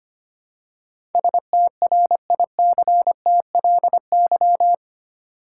Morse code, heard as a keyed tone, about 25 wpm, 700 Hz. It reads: STRICTLY
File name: Eight Letter Words - 25wpm